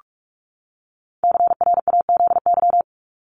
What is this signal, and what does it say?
Morse code, heard as a keyed tone, about 44 wpm, 700 Hz. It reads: CRAZY
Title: Top 1501-1750 Words 45wpm